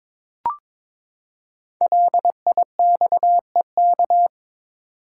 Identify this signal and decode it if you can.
Morse code, heard as a keyed tone, about 22 wpm, 700 Hz. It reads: LIXEK
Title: Code Groups - 5 Letter:Numbers - 22wpm